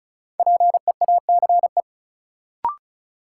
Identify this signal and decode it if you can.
Morse code, heard as a keyed tone, about 36 wpm, 700 Hz. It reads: PEACE
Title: Sentences from Top 1500 Words - New Word Drill - 35wpm